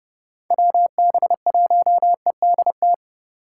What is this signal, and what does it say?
Morse code, heard as a keyed tone, about 31 wpm, 700 Hz. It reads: WB1EDT